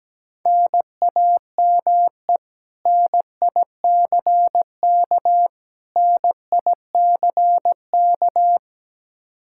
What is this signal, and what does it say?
Morse code, heard as a keyed tone, about 17 wpm, 700 Hz. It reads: NAME NICK NICK
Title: Practice QSO - Exchange 1 - Beginning - 17wpm